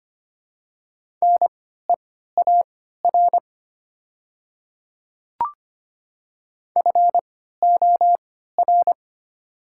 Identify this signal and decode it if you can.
Morse code, heard as a keyed tone, about 25 wpm, spaced as for 14 wpm, 700 Hz. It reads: NEAR FOR
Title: 3x Character Spacing - Top 200 Words - 25wpm